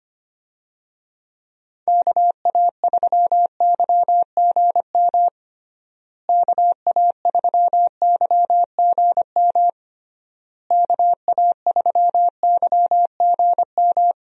Morse code, heard as a keyed tone, about 25 wpm, 700 Hz. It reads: KA3YGM KA3YGM KA3YGM
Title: USA Call Signs - 6 Characters - Repeated 3x - 25wpm